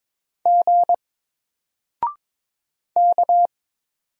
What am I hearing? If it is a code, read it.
Morse code, heard as a keyed tone, about 22 wpm, 700 Hz. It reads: G K
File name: Single-Letter-Number-22wpm